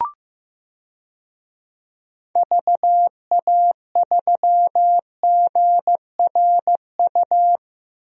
Morse code, heard as a keyed tone, about 15 wpm, 700 Hz. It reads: VA3GRU
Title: DXCC-100 Call Signs - 6 Characters - Repeated 2x - 15wpm